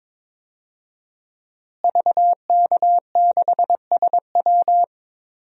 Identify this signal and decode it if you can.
Morse code, heard as a keyed tone, about 22 wpm, 700 Hz. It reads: VK6SW
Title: DXCC-100 Call Signs - 5 Characters - Repeated 2x - 22wpm